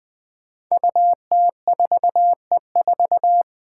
Morse code, heard as a keyed tone, about 20 wpm, 700 Hz. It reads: UT4E4
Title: Five Challenging Letter-Number Combinations 20wpm